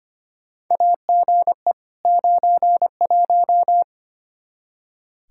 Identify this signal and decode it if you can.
Morse code, heard as a keyed tone, about 25 wpm, 700 Hz. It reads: AGE 91
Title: Practice QSO - Exchange 3 - More Info - 25wpm